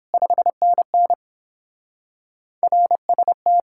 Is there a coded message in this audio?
Yes: Morse code from 0.1 to 3.6 s, heard as a keyed tone, about 29 wpm, 700 Hz. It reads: 5NN RST